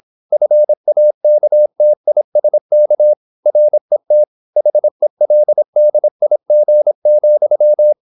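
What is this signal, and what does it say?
Morse code, heard as a keyed tone, about 26 wpm, 600 Hz. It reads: FAKTISK RET HELDIG,
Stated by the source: High-speed (QRQ) Morse at 26 wpm.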